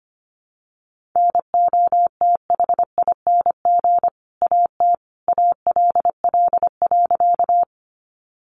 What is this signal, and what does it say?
Morse code, heard as a keyed tone, about 25 wpm, 700 Hz. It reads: NOTHING AT ALL.
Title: Sentences from Top 500 Words 25wpm